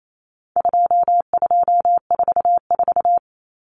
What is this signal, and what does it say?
Morse code, heard as a keyed tone, about 28 wpm, 700 Hz. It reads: 2244
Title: Common Cut Numbers - 4-Digit 28wpm